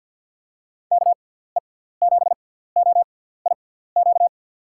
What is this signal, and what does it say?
Morse code, heard as a keyed tone, about 53 wpm, spaced as for 17 wpm, 700 Hz. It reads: KE7OIQ
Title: USA Call Signs - 6 Characters - 6x Character Spacing - 50wpm